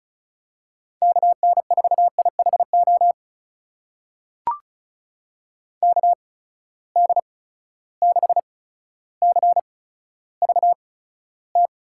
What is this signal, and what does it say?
Morse code, heard as a keyed tone, about 36 wpm, spaced as for 15 wpm, 700 Hz. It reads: KN4IHO KD6CVT
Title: USA Call Signs - 6 Characters - 8x Character Spacing - 35wpm